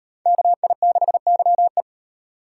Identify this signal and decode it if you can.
Morse code, heard as a keyed tone, about 37 wpm, 700 Hz. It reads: KI6YE